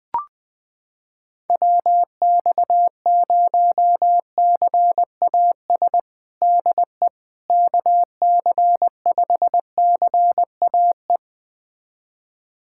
Morse code, heard as a keyed tone, about 20 wpm, 700 Hz. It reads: WX0CAS DE KC5CAE
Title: USA Callsign DE Callsign - 6 Characters - 20wpm